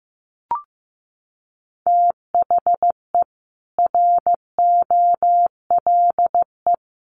Morse code, heard as a keyed tone, about 15 wpm, 700 Hz. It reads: THE ROLE
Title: Sets of 3 Words 15wpm